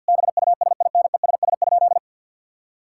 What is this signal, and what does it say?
Morse code, heard as a keyed tone, about 50 wpm, 700 Hz. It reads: BUSINESS?
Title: Sentences from Top 1000 Words - Review Part 6 - 50wpm